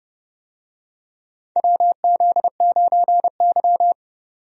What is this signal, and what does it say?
Morse code, heard as a keyed tone, about 29 wpm, 700 Hz. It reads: WZ9Y
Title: USA Call Signs - 4 Characters - 30wpm